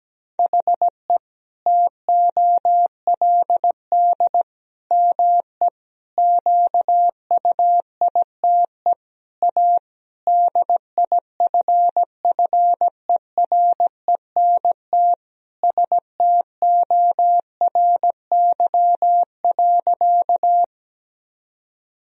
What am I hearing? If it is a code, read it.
Morse code, heard as a keyed tone, about 17 wpm, 700 Hz. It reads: HE TOLD ME QUITE A DIFFERENT STORY.
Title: Sentences from Top 1000 Words - Review Part 3 - 17wpm